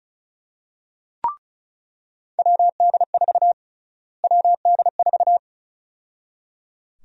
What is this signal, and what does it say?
Morse code, heard as a keyed tone, about 36 wpm, 700 Hz. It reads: WD4 WD4
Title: USA Call Sign Prefixes - Repeated 2x - 35wpm